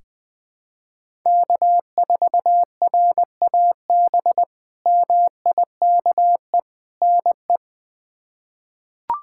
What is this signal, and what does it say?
Morse code, heard as a keyed tone, about 20 wpm, 700 Hz. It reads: K4RAB MIKE NE
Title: K1USN SST QSO Practice - 20wpm